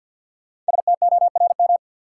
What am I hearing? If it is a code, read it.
Morse code, heard as a keyed tone, about 51 wpm, 700 Hz. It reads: STORM
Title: Sentences from Top 1500 Words - New Word Drill Alt - 50wpm